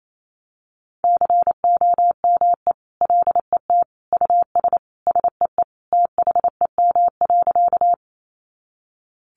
Morse code, heard as a keyed tone, about 28 wpm, 700 Hz. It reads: COME LET US SEE THEM.
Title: Sentences from Top 300 Words 28wpm